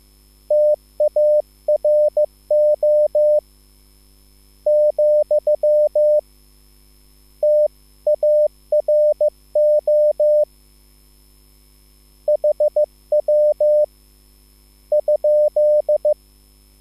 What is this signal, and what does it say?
Morse code, heard as a keyed tone, about 15 wpm, 600 Hz. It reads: TARO , TARO HW ?